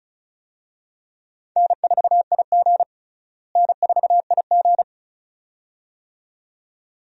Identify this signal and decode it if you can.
Morse code, heard as a keyed tone, about 36 wpm, 700 Hz. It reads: N4IG N4IG
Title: USA Call Signs - 4 Characters - Repeated 2x - 35wpm